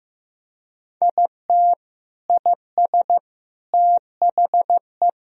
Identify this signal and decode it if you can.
Morse code, heard as a keyed tone, about 15 wpm, 700 Hz. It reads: IT IS THE
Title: Sentences from Top 1000 Words - Review Part 2 - 15wpm